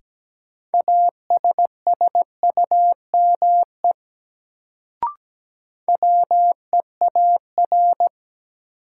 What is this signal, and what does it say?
Morse code, heard as a keyed tone, about 17 wpm, 700 Hz. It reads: ASSUME WEAR
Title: Sentences from Top 1500 Words - New Word Drill Alt - 17wpm